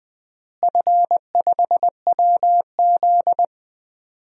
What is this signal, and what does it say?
Morse code, heard as a keyed tone, about 20 wpm, 700 Hz. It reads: F5WZ
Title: DXCC-100 Call Signs - 4 Characters - Repeated 3x - 20wpm